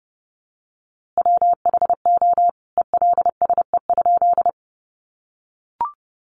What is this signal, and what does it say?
Morse code, heard as a keyed tone, about 30 wpm, 700 Hz. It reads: WHO ELSE?
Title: A Sound Of Thunder 30wpm